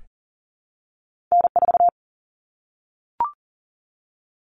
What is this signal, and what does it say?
Morse code, heard as a keyed tone, about 40 wpm, 700 Hz. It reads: N4
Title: Common Cut Numbers - 2-Digit 40wpm